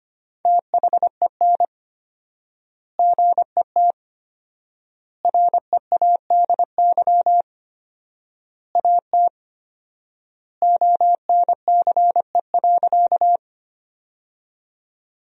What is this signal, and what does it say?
Morse code, heard as a keyed tone, about 25 wpm, 700 Hz. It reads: THEN GET READY AT ONCE.
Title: 4x Word-Spacing - Sentences from Top 400 Words - 25wpm